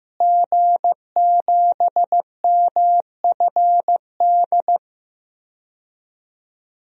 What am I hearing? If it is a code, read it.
Morse code, heard as a keyed tone, about 15 wpm, 700 Hz. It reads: G7MFD
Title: DXCC-100 Call Signs - 5 Characters - Repeated 3x - 15wpm